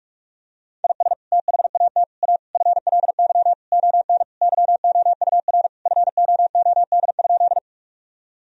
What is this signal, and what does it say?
Morse code, heard as a keyed tone, about 46 wpm, 700 Hz. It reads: IS THAT A FLY ON YOUR FOOD?
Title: Sentences from Top 1000 Words - Review Part 2 - 45wpm